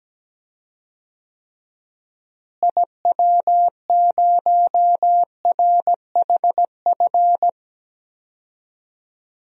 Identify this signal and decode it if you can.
Morse code, heard as a keyed tone, about 17 wpm, 700 Hz. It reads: IW0RHF
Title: DXCC-100 Call Signs - 6 Characters - Repeated 3x - 17wpm